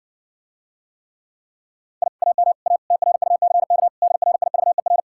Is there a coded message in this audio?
Yes: Morse code from 2.0 to 5.0 s, heard as a keyed tone, about 60 wpm, 700 Hz. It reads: I AM A TRUCK DRIVER